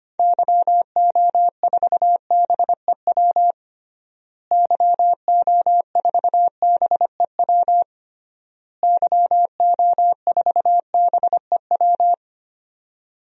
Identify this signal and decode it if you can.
Morse code, heard as a keyed tone, about 25 wpm, 700 Hz. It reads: YO4BEW YO4BEW YO4BEW
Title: DXCC-100 Call Signs - 6 Characters - Repeated 3x - 25wpm